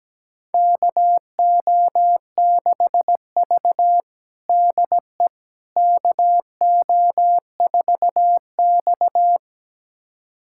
Morse code, heard as a keyed tone, about 17 wpm, 700 Hz. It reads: KO6V DE KO4X
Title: USA Callsign DE Callsign - 4 Characters - 17wpm